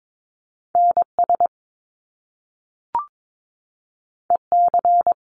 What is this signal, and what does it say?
Morse code, heard as a keyed tone, about 22 wpm, 700 Hz. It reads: NS EC
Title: Top Two Letter Combinations in Words 22wpm